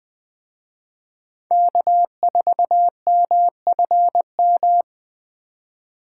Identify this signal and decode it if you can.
Morse code, heard as a keyed tone, about 19 wpm, 700 Hz. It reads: K4MFM